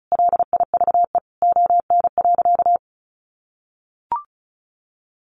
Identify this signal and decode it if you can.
Morse code, heard as a keyed tone, about 34 wpm, 700 Hz. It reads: LIVE ON.
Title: Sentences from Top 800 Words 35wpm